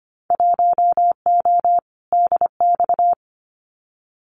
Morse code, heard as a keyed tone, about 25 wpm, 700 Hz. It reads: 1O DX
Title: Winter Field Day Exchanges Encore 25wpm